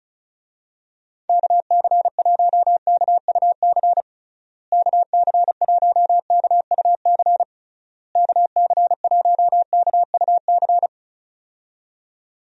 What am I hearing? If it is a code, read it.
Morse code, heard as a keyed tone, about 36 wpm, 700 Hz. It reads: KC1KUC KC1KUC KC1KUC
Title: USA Call Signs - 6 Characters - Repeated 3x - 35wpm